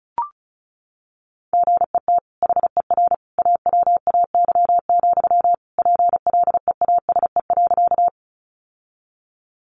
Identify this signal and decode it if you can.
Morse code, heard as a keyed tone, about 34 wpm, 700 Hz. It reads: GET HER AWAY, PLEASE.
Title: Sentences from Top 1500 Words 35wpm